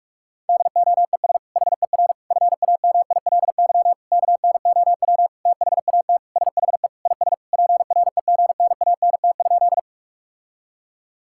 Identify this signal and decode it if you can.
Morse code, heard as a keyed tone, about 46 wpm, 700 Hz. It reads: DOES HER FAMILY KNOW THAT SHE IS PREGNANT?
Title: Sentences from Top 1000 Words - Review Part 6 - 45wpm